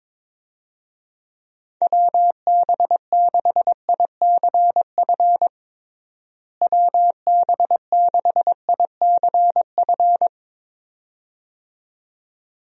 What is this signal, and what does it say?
Morse code, heard as a keyed tone, about 22 wpm, 700 Hz. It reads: WB6ICF WB6ICF
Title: USA Call Signs - 6 Characters - Repeated 2x - 22wpm